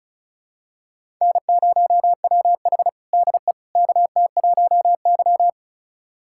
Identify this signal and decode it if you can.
Morse code, heard as a keyed tone, about 34 wpm, 700 Hz. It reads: N0WH DE KT1Y
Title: USA Callsign DE Callsign - 4 Characters - 35wpm